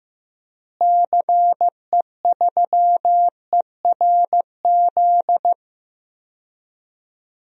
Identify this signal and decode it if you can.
Morse code, heard as a keyed tone, about 16 wpm, 700 Hz. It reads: CE3ERZ
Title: DXCC-100 Call Signs - 6 Characters - Repeated 3x - 15wpm